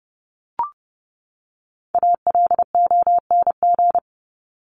Morse code, heard as a keyed tone, about 30 wpm, 700 Hz. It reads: ALONG
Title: Top 501-750 Words 30wpm